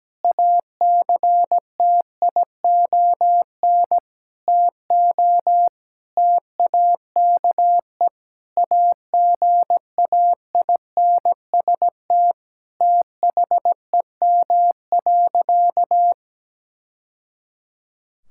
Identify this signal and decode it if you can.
Morse code, heard as a keyed tone, about 17 wpm, 700 Hz. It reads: ACTION TO TAKE AGAINST THEM.
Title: Sentences from Top 1000 Words - Review Part 6 - 17wpm